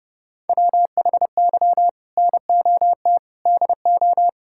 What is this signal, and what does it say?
Morse code, heard as a keyed tone, about 30 wpm, 700 Hz. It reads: WHY NOT DO
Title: Sentences from Top 1000 Words - Review Part 6 - 30wpm